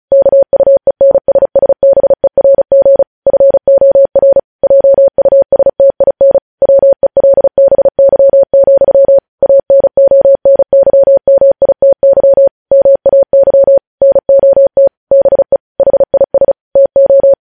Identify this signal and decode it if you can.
Morse code, heard as a keyed tone, about 36 wpm, 550 Hz. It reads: KUENSSBERG FOR JUSTIN WELBY, ANONYMITY MAY NOT BE HIS TO
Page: News headlines in Morse code at 35 wpm